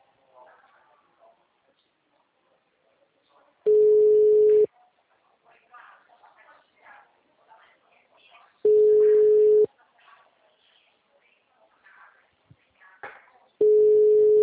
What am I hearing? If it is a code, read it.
Ringback tone (Europe, 425 Hz), with speech.
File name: out- -2025- - - .18918